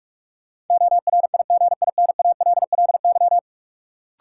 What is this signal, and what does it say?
Morse code, heard as a keyed tone, about 46 wpm, 700 Hz. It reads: ORIGINALLY
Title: Sentences from Top 1500 Words - New Word Drill - 45wpm